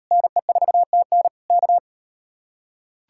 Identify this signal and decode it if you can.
Morse code, heard as a keyed tone, about 38 wpm, 700 Hz. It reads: NE4TN K